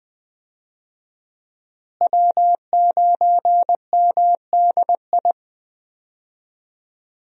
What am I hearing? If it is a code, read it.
Morse code, heard as a keyed tone, about 19 wpm, 700 Hz. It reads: W9MDI
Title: USA Call Signs - 5 Characters - Repeated 3x - 20wpm